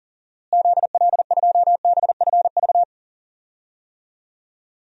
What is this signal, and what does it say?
Morse code, heard as a keyed tone, about 40 wpm, 700 Hz. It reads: ZL2BFV
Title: DXCC-100 Call Signs - 6 Characters - Repeated 3x - 40wpm